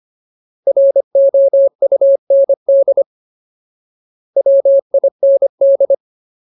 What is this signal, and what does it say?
Morse code, heard as a keyed tone, about 25 wpm, 550 Hz. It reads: ROUND WIND
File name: Ultimate Word List / 25WPM Ultimate